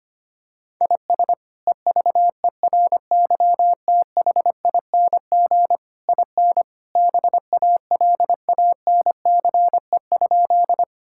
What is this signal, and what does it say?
Morse code, heard as a keyed tone, about 25 wpm, 700 Hz. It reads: IS EVERYTHING IN BALANCE?
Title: Sentences from Top 1000 Words - Review Part 6 - 25wpm